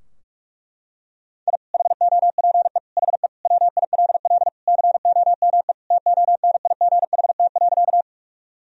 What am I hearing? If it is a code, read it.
Morse code, heard as a keyed tone, about 41 wpm, 700 Hz. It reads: I HOPE HE WILL COME TONIGHT.
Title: Sentences from Top 1000 Words - Review Part 3 - 45wpm